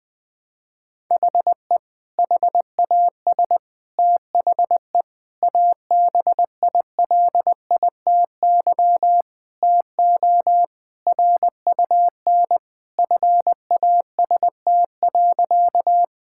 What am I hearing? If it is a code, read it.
Morse code, heard as a keyed tone, about 20 wpm, 700 Hz. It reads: HE HAS THE ABILITY TO RUN FAST.
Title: Sentences with 3-Word Phrases - Spoken Beginning - 20wpm